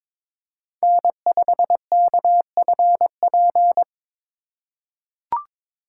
Morse code, heard as a keyed tone, about 22 wpm, 700 Hz. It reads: N5KFP
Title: USA Call Signs - 5 Characters - 2x Character Spacing - 22wpm